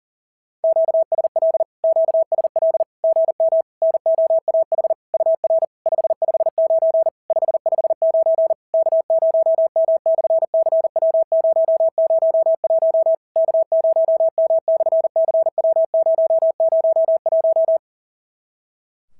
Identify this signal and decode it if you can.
Morse code, heard as a keyed tone, about 40 wpm, 650 Hz. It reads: QSL QSL GM NOAH UR 559 559 K0M/CW001 K0M/CW001
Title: SOTA QSO Practice 40wpm